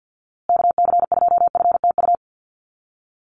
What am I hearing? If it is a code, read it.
Morse code, heard as a keyed tone, about 50 wpm, 700 Hz. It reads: KC2FTU
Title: USA Call Signs Repeated 2x 50wpm